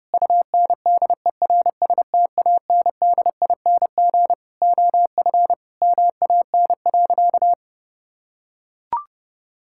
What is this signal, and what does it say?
Morse code, heard as a keyed tone, about 32 wpm, 700 Hz. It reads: UNDERSTANDING OF MAN.
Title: Sentences from Top 1000 Words - Review Part 5 - 30wpm